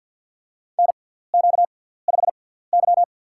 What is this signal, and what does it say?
Morse code, heard as a keyed tone, about 53 wpm, spaced as for 17 wpm, 690 Hz. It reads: NQ5Y
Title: USA Call Signs - 4 Characters - 6x Character Spacing - 50wpm